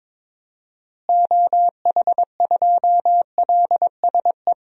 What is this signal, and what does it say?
Morse code, heard as a keyed tone, about 22 wpm, 700 Hz. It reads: OH2LSE
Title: DXCC-100 Call Signs - 6 Characters - Repeated 3x - 22wpm